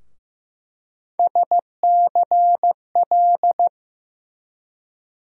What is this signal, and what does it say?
Morse code, heard as a keyed tone, about 15 wpm, 700 Hz. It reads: SCL
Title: USA Call Sign Postfixes - 3 Letters 15wpm